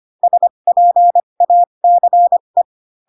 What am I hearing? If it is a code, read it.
Morse code, heard as a keyed tone, about 25 wpm, spaced as for 22 wpm, 700 Hz. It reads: SPACE